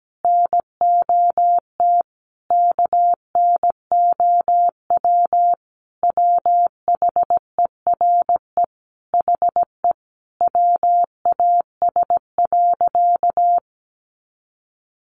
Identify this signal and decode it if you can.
Morse code, heard as a keyed tone, about 17 wpm, 700 Hz. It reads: NOT KNOW WHERE HE WAS.
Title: Sentences from Top 600 Words 17wpm